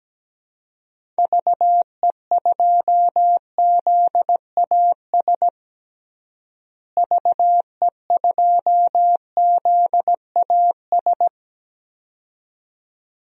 Morse code, heard as a keyed tone, about 17 wpm, 700 Hz. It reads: VE2ZAS VE2ZAS
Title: DXCC-100 Call Signs - 6 Characters - Repeated 2x - 17wpm